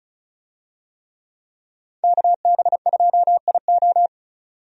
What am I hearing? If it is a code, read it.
Morse code, heard as a keyed tone, about 34 wpm, 700 Hz. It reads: KB2IO
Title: USA Call Signs - 5 Characters - 35wpm